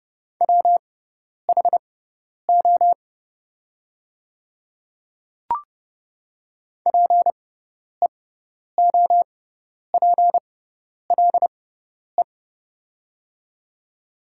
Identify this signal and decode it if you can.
Morse code, heard as a keyed tone, about 30 wpm, spaced as for 10 wpm, 690 Hz. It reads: WHO PEOPLE
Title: 6x Character Spacing - Top 100 Words - 30wpm